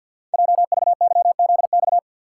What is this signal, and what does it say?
Morse code, heard as a keyed tone, about 50 wpm, 690 Hz. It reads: JVYZX
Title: Code Groups - 5 Letters - 50wpm